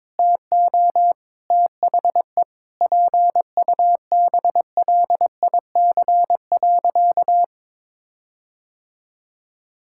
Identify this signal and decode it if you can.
Morse code, heard as a keyed tone, about 22 wpm, 700 Hz. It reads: TO THE PUBLIC.
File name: Sentences with 7-Word Phrases - 22wpm